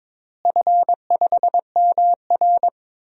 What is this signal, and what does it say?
Morse code, heard as a keyed tone, about 21 wpm, 700 Hz. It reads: F5MR
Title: DXCC-100 Call Signs - 4 Characters - Repeated 3x - 22wpm